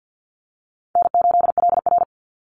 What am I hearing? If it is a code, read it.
Morse code, heard as a keyed tone, about 50 wpm, 700 Hz. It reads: N8LR
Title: Easy Call Signs - 50wpm